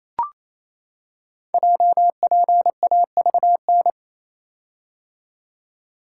Morse code, heard as a keyed tone, about 27 wpm, 700 Hz. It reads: JPAVN
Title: Code Groups - 5 Letters - 28wpm